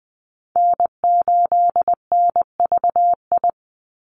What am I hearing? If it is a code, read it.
Morse code, heard as a keyed tone, about 20 wpm, 700 Hz. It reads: N8NVI
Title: USA Call Signs 20wpm